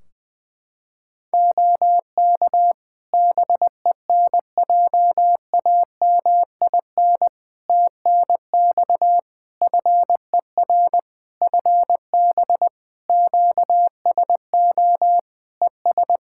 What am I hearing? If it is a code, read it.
Morse code, heard as a keyed tone, about 20 wpm, 700 Hz. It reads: OK BENJAMIN TNX FER FB QSO ES
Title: Practice QSO - Exchange 4 - Ending - 20wpm